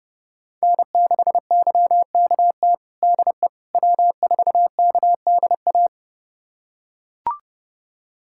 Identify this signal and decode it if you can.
Morse code, heard as a keyed tone, about 32 wpm, 700 Hz. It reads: N6YKT DE W4KDA
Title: USA Callsign DE Callsign - 5 Characters - 30wpm